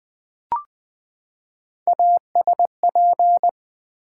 Morse code, heard as a keyed tone, about 21 wpm, 700 Hz. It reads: ASP
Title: Build A Word - 3 Letters At A Time - 6-Letter Words - 20wpm